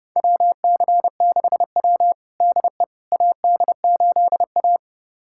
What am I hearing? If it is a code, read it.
Morse code, heard as a keyed tone, about 30 wpm, 700 Hz. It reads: WC6W DE AD8A
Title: USA Callsign DE Callsign - 4 Characters - 30wpm